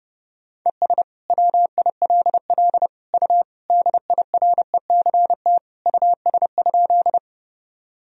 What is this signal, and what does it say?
Morse code, heard as a keyed tone, about 30 wpm, 700 Hz. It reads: ES WILL U DIRECT US?
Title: Sentences from Top 400 Words and Abbreviations 30wpm